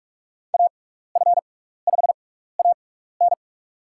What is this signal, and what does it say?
Morse code, heard as a keyed tone, about 44 wpm, spaced as for 15 wpm, 690 Hz. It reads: AF5AN
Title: USA Call Signs - 5 Characters - 6x Character Spacing - 45wpm